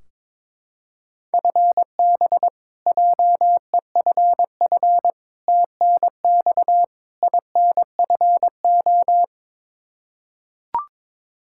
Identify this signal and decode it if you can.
Morse code, heard as a keyed tone, about 22 wpm, 700 Hz. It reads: FB JEFF TNX INFO
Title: Practice QSO - Exchange 3 - More Info - 22wpm